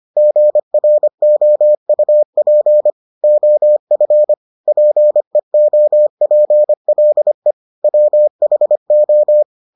Morse code, heard as a keyed tone, about 25 wpm, 600 Hz. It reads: GROUP OF PEOPLE WHO